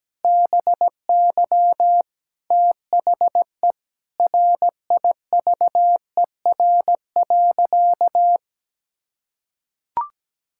Morse code, heard as a keyed tone, about 17 wpm, 700 Hz. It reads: BY THE RIVER.
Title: Sentences from Top 1600 Words 17wpm